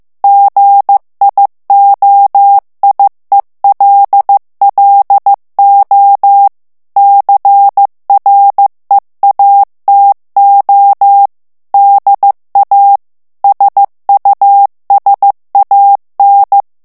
Morse code, heard as a keyed tone, about 15 wpm, 800 Hz. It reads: GIOIELLO CREATO DA SUSAN
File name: ITALIEN texte clair 1 vitesse 14 mots minute